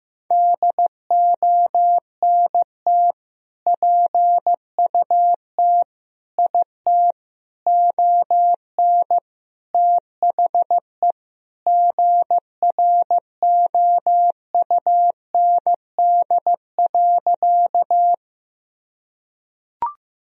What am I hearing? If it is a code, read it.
Morse code, heard as a keyed tone, about 16 wpm, 700 Hz. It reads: DONT PUT IT ON THE GROUND.
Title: Sentences from Top 1000 Words - Review Part 2 - 15wpm